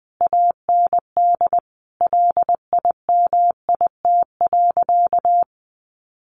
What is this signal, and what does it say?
Morse code, heard as a keyed tone, about 19 wpm, 700 Hz. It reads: AND LIMIT.
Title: Sentences from Top 1500 Words 20wpm